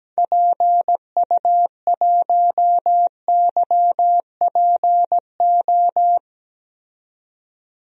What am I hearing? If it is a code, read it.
Morse code, heard as a keyed tone, about 17 wpm, 700 Hz. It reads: PU1YPO
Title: DXCC-100 Call Signs - 6 Characters - 17wpm